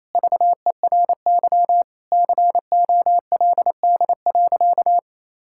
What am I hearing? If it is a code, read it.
Morse code, heard as a keyed tone, about 28 wpm, 700 Hz. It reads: VERY COLD.
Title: Sentences from Top 1000 Words - Review Part 5 - 28wpm